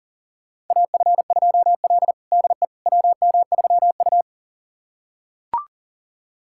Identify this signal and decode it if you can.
Morse code, heard as a keyed tone, about 40 wpm, 700 Hz. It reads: AF2L DE WM3U
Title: USA Callsign DE Callsign - 4 Characters - 40wpm